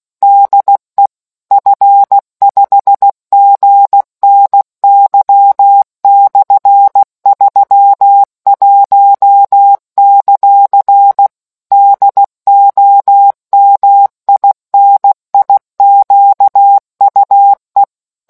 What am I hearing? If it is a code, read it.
Morse code, heard as a keyed tone, about 16 wpm, 800 Hz. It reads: DE F5GNY/31; DOMINIQUE